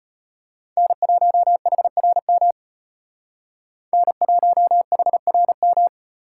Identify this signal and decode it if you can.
Morse code, heard as a keyed tone, about 32 wpm, 700 Hz. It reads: N1HRM N1HRM